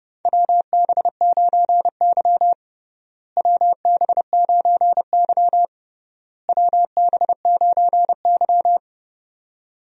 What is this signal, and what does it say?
Morse code, heard as a keyed tone, about 30 wpm, 700 Hz. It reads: WB9Y WB9Y WB9Y
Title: USA Call Signs - 4 Characters - Repeated 3x - 30wpm